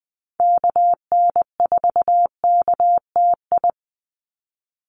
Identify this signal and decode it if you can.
Morse code, heard as a keyed tone, about 20 wpm, 700 Hz. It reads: KN4KTI
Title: USA Call Signs 20wpm